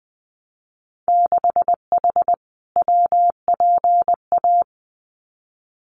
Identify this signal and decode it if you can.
Morse code, heard as a keyed tone, about 20 wpm, 700 Hz. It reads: 6H WPA
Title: Winter Field Day Exchanges Encore 20wpm